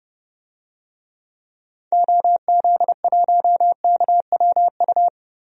Morse code, heard as a keyed tone, about 31 wpm, 700 Hz. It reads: OZ1KWU
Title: DXCC-100 Call Signs - 6 Characters - 30wpm